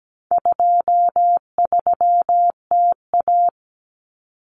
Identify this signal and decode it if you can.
Morse code, heard as a keyed tone, about 17 wpm, 700 Hz. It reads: 23TA
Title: Common Cut Numbers - 4-Digit 17wpm